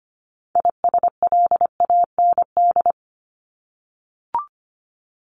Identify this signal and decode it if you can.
Morse code, heard as a keyed tone, about 25 wpm, 700 Hz. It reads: ISLAND
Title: Top 500 Words 25wpm